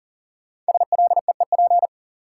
Morse code, heard as a keyed tone, about 40 wpm, 700 Hz. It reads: SLEEP
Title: Sentences from Top 500 Words - New Word Drill - 40wpm